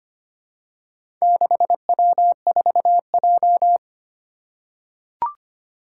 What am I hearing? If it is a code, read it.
Morse code, heard as a keyed tone, about 25 wpm, 700 Hz. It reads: 6W4J
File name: practice / 25wpm / Four Letter-Number Combinations - 25wpm